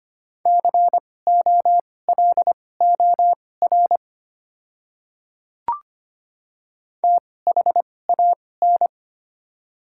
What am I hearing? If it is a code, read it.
Morse code, heard as a keyed tone, about 25 wpm, spaced as for 18 wpm, 700 Hz. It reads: COLOR THAN
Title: 2x Character Spacing - Top 300 Words - 25wpm